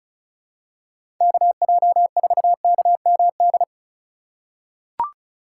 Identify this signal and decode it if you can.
Morse code, heard as a keyed tone, about 36 wpm, 700 Hz. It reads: KJ4KMD
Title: USA Call Signs - 6 Characters - 4x Character Spacing - 35wpm